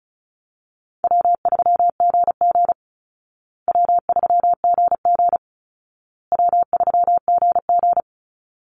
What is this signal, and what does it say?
Morse code, heard as a keyed tone, about 34 wpm, 700 Hz. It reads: W3GG W3GG W3GG
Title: Easy Call Signs Repeated 3x - 35wpm